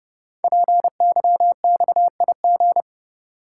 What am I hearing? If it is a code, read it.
Morse code, heard as a keyed tone, about 30 wpm, 700 Hz. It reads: PYXIG
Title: Code Groups - 5 Letters - 30wpm